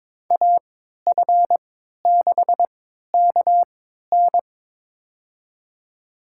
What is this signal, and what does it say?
Morse code, heard as a keyed tone, about 22 wpm, spaced as for 13 wpm, 700 Hz. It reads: AF6KN
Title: USA Call Signs - 3x Character Spacing - 22wpm